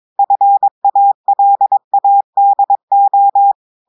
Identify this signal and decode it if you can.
Morse code, heard as a keyed tone, about 22 wpm, 800 Hz. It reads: FALADO
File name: audios / portugais / portugais texte en clair 2 vitesse 22 mots minutes